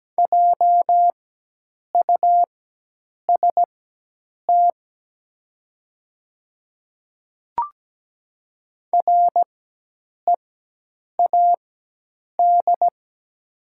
Morse code, heard as a keyed tone, about 17 wpm, spaced as for 8 wpm, 700 Hz. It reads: JUST READ
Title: 4x Character Spacing - Top 200 Words - 17wpm